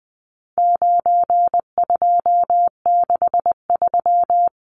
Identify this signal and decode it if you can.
Morse code, heard as a keyed tone, about 20 wpm, 700 Hz. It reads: 9263
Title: Numbers - 4-Digit 20wpm